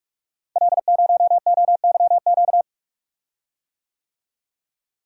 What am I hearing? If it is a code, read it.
Morse code, heard as a keyed tone, about 44 wpm, 700 Hz. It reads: L0OYQ
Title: Code Groups - 5 Letter:Numbers - 45wpm